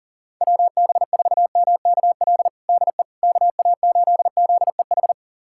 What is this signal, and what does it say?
Morse code, heard as a keyed tone, about 40 wpm, 700 Hz. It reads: WB4MKL DE KA8ZEH
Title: USA Callsign DE Callsign - 6 Characters - 40wpm